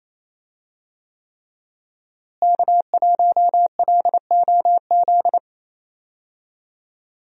Morse code, heard as a keyed tone, about 28 wpm, 700 Hz. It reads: K1LOZ
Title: USA Call Signs - 5 Characters - Repeated 3x - 28wpm